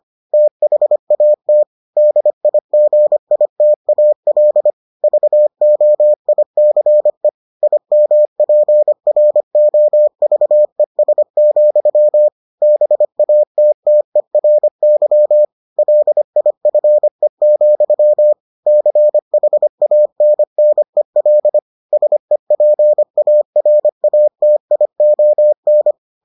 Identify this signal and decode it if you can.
Morse code, heard as a keyed tone, about 25 wpm, 600 Hz. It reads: THAT DIGITAL VOICE IMPROVES, BATTERY LIFE, CHANNEL SEPARATION